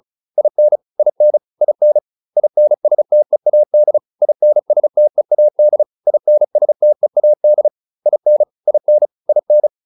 Morse code, heard as a keyed tone, about 34 wpm, 600 Hz. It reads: IN IN IN INSTEAD INSTEAD INSTEAD IN IN IN
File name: IN7 35